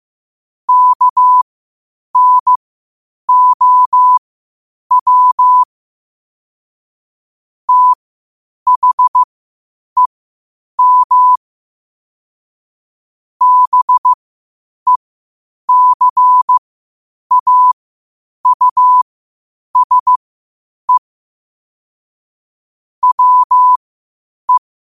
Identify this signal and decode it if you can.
Morse code, heard as a keyed tone, about 15 wpm, spaced as for 8 wpm, 1000 Hz. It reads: KNOW THEM BECAUSE WE
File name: qotdfiles / 8wpm